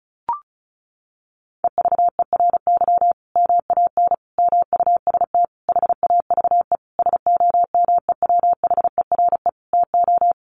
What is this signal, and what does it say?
Morse code, heard as a keyed tone, about 34 wpm, 700 Hz. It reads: EVERY MAN MUST HAVE SOMEWHERE TO
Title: Sentences from Top 1400 Words 35wpm